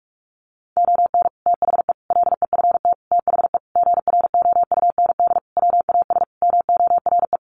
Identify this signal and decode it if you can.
Morse code, heard as a keyed tone, about 44 wpm, 700 Hz. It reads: ON THE LEFT THE GROUND WAS MORE
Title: Sentences from Top 700 Words 45wpm